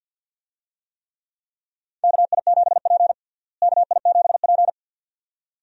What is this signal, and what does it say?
Morse code, heard as a keyed tone, about 46 wpm, 700 Hz. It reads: KI7P KI7P
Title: USA Call Signs - 4 Characters - Repeated 2x - 50wpm